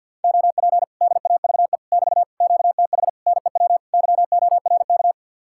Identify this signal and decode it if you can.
Morse code, heard as a keyed tone, about 67 wpm, spaced as for 58 wpm, 700 Hz. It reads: OP DAVE <BT> QTH NEW YORK
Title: Speed-Racing Common QSO Phrases - 50wpm